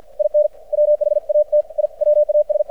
Morse code, heard as a keyed tone, about 53 wpm, 600 Hz. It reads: IN WHAT I WAS
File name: archived / CRSnet